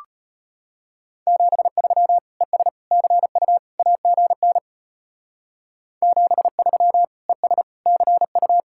Morse code, heard as a keyed tone, about 33 wpm, 700 Hz. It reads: 73 ES CU AGN 73 ES CU